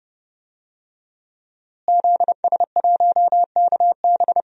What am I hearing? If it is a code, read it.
Morse code, heard as a keyed tone, about 32 wpm, 700 Hz. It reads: ZS1KB
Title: DXCC-100 Call Signs - 5 Characters - Repeated 2x - 30wpm